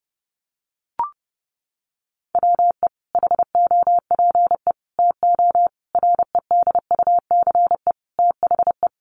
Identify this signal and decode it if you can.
Morse code, heard as a keyed tone, about 30 wpm, 700 Hz. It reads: WE HOPE TO REDUCE THE
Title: Sentences from Top 1100 Words 30wpm